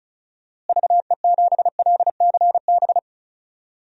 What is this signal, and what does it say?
Morse code, heard as a keyed tone, about 34 wpm, 700 Hz. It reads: VE7LCB
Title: DXCC-100 Call Signs - 6 Characters - Repeated 3x - 35wpm